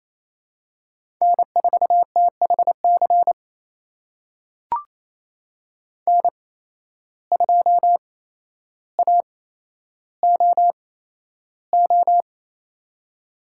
Morse code, heard as a keyed tone, about 28 wpm, spaced as for 12 wpm, 700 Hz. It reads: N4THC N2AOO
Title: USA Call Signs - 5 Characters - 8x Character Spacing - 28wpm